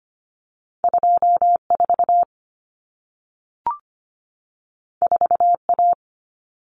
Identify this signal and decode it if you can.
Morse code, heard as a keyed tone, about 25 wpm, 700 Hz. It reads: 24 4A
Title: Common Cut Numbers - 2-Digit 25wpm